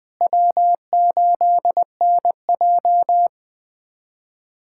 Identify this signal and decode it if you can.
Morse code, heard as a keyed tone, about 20 wpm, 700 Hz. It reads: W8NJ
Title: USA Callsign DE Callsign - 4 Characters - 20wpm